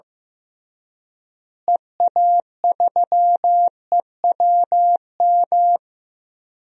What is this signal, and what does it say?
Morse code, heard as a keyed tone, about 16 wpm, 700 Hz. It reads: EA3EWM
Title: DXCC-100 Call Signs - 6 Characters - Repeated 3x - 15wpm